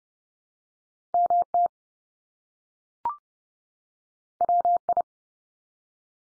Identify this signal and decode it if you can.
Morse code, heard as a keyed tone, about 30 wpm, 700 Hz. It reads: MT WI
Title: US States 30WPM